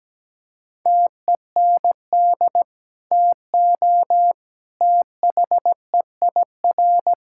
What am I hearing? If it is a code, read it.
Morse code, heard as a keyed tone, about 17 wpm, 700 Hz. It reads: TEND TO THEIR
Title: Sentences from Top 1000 Words - Review Part 5 - 17wpm